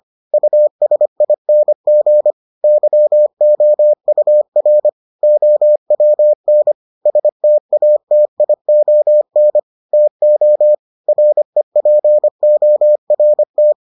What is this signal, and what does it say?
Morse code, heard as a keyed tone, about 25 wpm, 600 Hz. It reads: USING YOUR OWN STATION TO REPORT